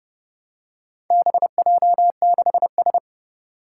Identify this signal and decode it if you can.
Morse code, heard as a keyed tone, about 30 wpm, 700 Hz. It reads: BJ6S
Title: Four Challenging Letter-Number Combinations 30wpm